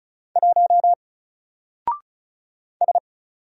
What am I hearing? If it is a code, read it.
Morse code, heard as a keyed tone, about 35 wpm, 700 Hz. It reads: 1 S
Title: Single Character (Letter:Number:Punctuation:Prosign) 35wpm